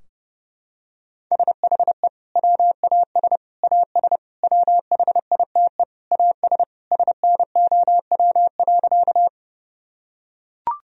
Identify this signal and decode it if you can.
Morse code, heard as a keyed tone, about 32 wpm, 700 Hz. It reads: SHE WAS AS WHITE AS SNOW.
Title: Sentences from Top 1000 Words - Review Part 3 - 30wpm